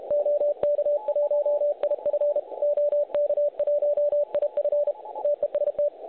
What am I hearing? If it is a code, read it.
Morse code, heard as a keyed tone, about 32 wpm, 590 Hz. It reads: OK1IF OK1IF TEST
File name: truSDX / websdr recording 3541.9kHz